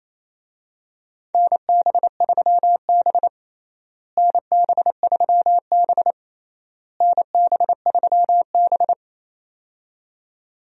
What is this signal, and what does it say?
Morse code, heard as a keyed tone, about 28 wpm, 700 Hz. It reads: NB3B NB3B NB3B
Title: USA Call Signs - 4 Characters - Repeated 3x - 28wpm